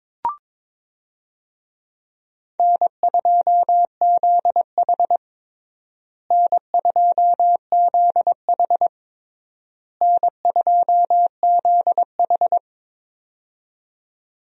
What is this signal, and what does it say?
Morse code, heard as a keyed tone, about 22 wpm, 700 Hz. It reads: N2ZH N2ZH N2ZH
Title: USA Call Signs - 4 Characters - Repeated 3x - 22wpm